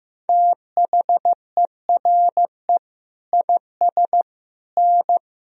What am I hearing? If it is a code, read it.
Morse code, heard as a keyed tone, about 15 wpm, 700 Hz. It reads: THERE IS N
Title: Sentences from Top 500 Words and Abbreviations 15wpm